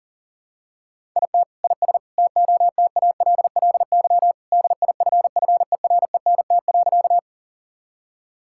Morse code, heard as a keyed tone, about 37 wpm, 700 Hz. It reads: IT IS TOTALLY DIFFERENT.
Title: Sentences from Top 1600 Words 40wpm